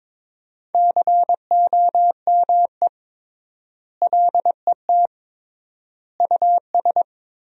Morse code, heard as a keyed tone, about 22 wpm, 700 Hz. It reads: COME LET US
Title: 3x Word-Spacing - Sentences from Top 300 Words - 22wpm